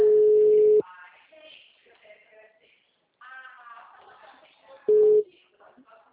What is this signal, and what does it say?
Ringback tone (Europe, 425 Hz), with speech.